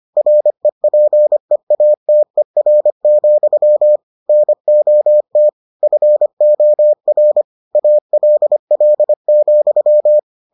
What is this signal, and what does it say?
Morse code, heard as a keyed tone, about 25 wpm, 600 Hz. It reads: REPEATER, NOT FOR ALL,